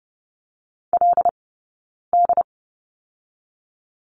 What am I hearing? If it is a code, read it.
Morse code, heard as a keyed tone, about 32 wpm, 700 Hz. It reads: L D
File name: 092 Lesson 16d - Letter c - Two Characters 30wpm